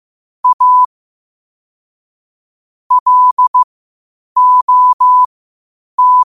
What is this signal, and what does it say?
Morse code, heard as a keyed tone, about 15 wpm, spaced as for 8 wpm, 1000 Hz. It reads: A LOT
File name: qotdfiles / 8wpm